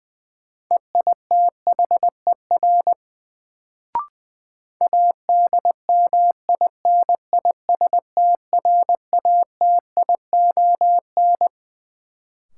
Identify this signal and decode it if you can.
Morse code, heard as a keyed tone, about 20 wpm, 700 Hz. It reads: EITHER ADMINISTRATION
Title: Sets of 5 Words - New Word Drill - 20wpm